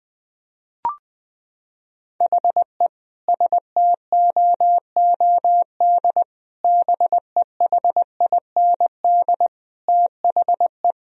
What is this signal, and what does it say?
Morse code, heard as a keyed tone, about 20 wpm, 700 Hz. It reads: HE STOOD BEHIND THE
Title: Sentences from Top 1000 Words - Review Part 2 - 20wpm